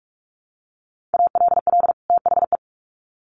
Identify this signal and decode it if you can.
Morse code, heard as a keyed tone, about 44 wpm, 700 Hz. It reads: ALL THE
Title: Sets of 2 Words 45wpm